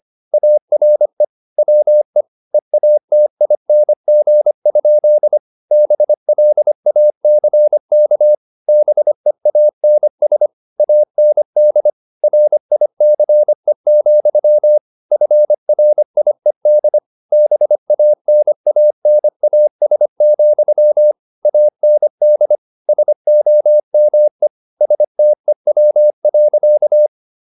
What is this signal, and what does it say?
Morse code, heard as a keyed tone, about 25 wpm, 600 Hz. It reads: ARE WE EATING? BLACK BEANS AND RICE, FRIED BANANAS, AND SOME STEW.
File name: the old man and the sea part 1 25wpm